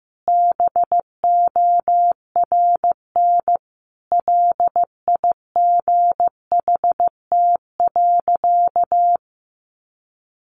Morse code, heard as a keyed tone, about 15 wpm, 700 Hz. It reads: BORN LIGHT.
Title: Sentences from Top 1100 Words 15wpm